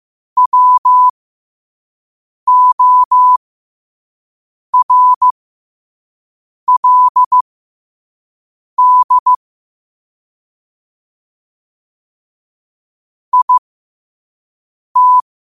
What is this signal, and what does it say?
Morse code, heard as a keyed tone, about 15 wpm, spaced as for 5 wpm, 1000 Hz. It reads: WORLD IT
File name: qotdfiles / 5wpm